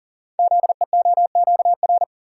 Morse code, heard as a keyed tone, about 38 wpm, 700 Hz. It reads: ZEOQR
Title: Code Groups - 5 Letters - 40wpm